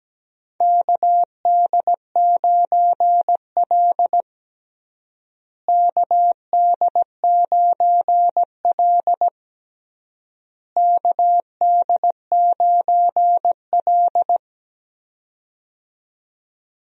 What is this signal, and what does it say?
Morse code, heard as a keyed tone, about 17 wpm, 700 Hz. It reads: KD9L KD9L KD9L
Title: USA Call Signs - 4 Characters - Repeated 3x - 17wpm